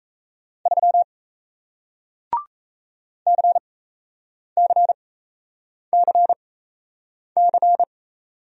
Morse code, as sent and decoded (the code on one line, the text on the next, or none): ...-- / -.-. -.-. -.-. -.-.
3 CCCC